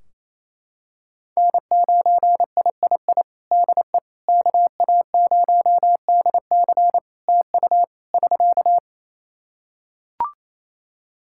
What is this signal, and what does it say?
Morse code, heard as a keyed tone, about 27 wpm, 700 Hz. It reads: N9III DE KA0DC TU <SK>
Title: Practice QSO - Exchange 4 - Ending - 28wpm